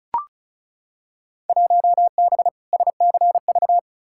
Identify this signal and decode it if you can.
Morse code, heard as a keyed tone, about 36 wpm, 700 Hz. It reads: 1B SCV
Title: Field Day Exchanges Encore - 35wpm